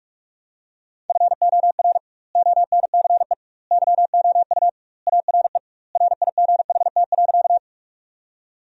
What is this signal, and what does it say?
Morse code, heard as a keyed tone, about 46 wpm, 700 Hz. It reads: FOR ONCE YOU ARE RIGHT.
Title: 2x Word-Spacing - Sentences from Top 300 Words - 45wpm